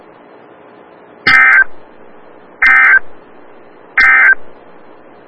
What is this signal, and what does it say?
Emergency Alert System (SAME) tones, heard as harsh buzzing bursts. It is heard from 1.1 to 4.4 s.
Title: Des Moines, IA NOAA Weather Radio - WXL57